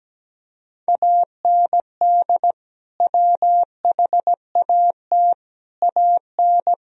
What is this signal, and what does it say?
Morse code, heard as a keyed tone, about 17 wpm, 700 Hz. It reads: AND WHAT AN
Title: Sentences from Top 1000 Words - Review Part 1 - 17wpm